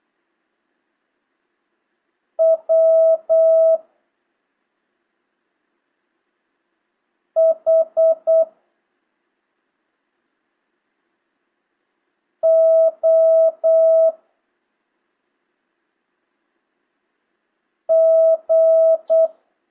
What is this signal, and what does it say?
Morse code, heard as a keyed tone, about 8 wpm, 650 Hz. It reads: W H O G
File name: jh8cbh / 2025cw